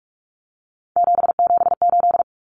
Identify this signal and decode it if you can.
Morse code, heard as a keyed tone, about 44 wpm, 700 Hz. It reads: 778
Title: Numbers - 3-Digit 45wpm